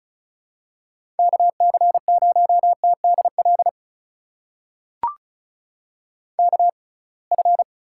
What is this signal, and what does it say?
Morse code, heard as a keyed tone, about 34 wpm, spaced as for 26 wpm, 700 Hz. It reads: KC0TDL KF
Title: USA Call Signs - 6 Characters - 6x Character Spacing - 35wpm